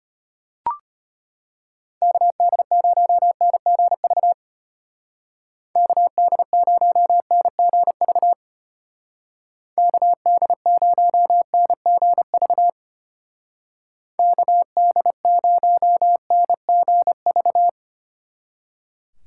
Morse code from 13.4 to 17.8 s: KD0NGV

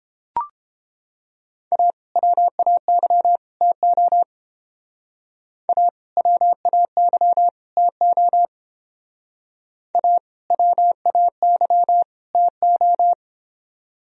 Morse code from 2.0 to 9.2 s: WAY TO A WAY TO

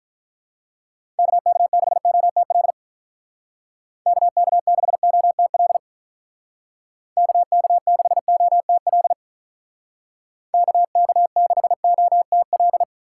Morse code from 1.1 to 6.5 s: KK6OTL KK6OTL

